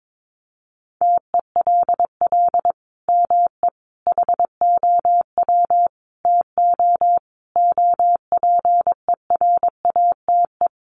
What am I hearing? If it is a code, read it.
Morse code, heard as a keyed tone, about 22 wpm, 700 Hz. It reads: TELL ME HOW TO OPERATE
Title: Sentences from Top 1500 Words 22wpm